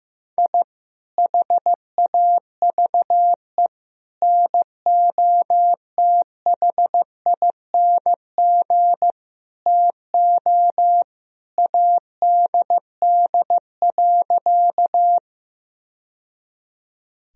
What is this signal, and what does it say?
Morse code, heard as a keyed tone, about 14 wpm, 700 Hz. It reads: I HAVE NOTHING TO ADD.
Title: Sentences from Top 1000 Words - Review Part 1 - 15wpm